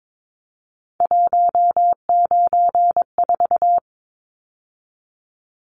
Morse code, heard as a keyed tone, about 22 wpm, 700 Hz. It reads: 194
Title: Numbers - 3-Digit 22wpm